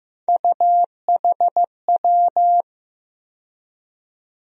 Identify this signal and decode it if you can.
Morse code, heard as a keyed tone, about 15 wpm, 700 Hz. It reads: UHW
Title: USA Call Sign Postfixes - 3 Letters 15wpm